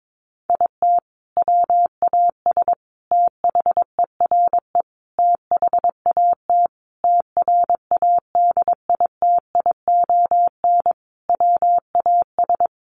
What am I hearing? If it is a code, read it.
Morse code, heard as a keyed tone, about 22 wpm, 700 Hz. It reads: IT WAS THERE THAT TRADITION WAS
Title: Sentences from Top 1500 Words 22wpm